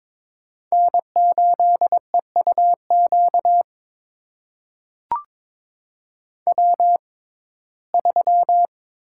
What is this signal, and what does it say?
Morse code, heard as a keyed tone, about 22 wpm, spaced as for 16 wpm, 700 Hz. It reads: N8EUQ W3
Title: USA Call Signs - 5 Characters - 6x Character Spacing - 22wpm